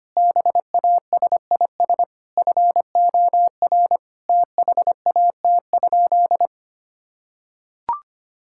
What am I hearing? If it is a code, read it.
Morse code, heard as a keyed tone, about 25 wpm, 700 Hz. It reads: BASIS FOR THAT?
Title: Sentences from Top 1000 Words - Review Part 5 - 25wpm